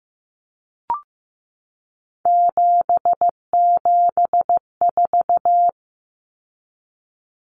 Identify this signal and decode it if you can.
Morse code, heard as a keyed tone, about 15 wpm, 700 Hz. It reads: 774
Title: Common Cut Numbers - 3-Digit 15wpm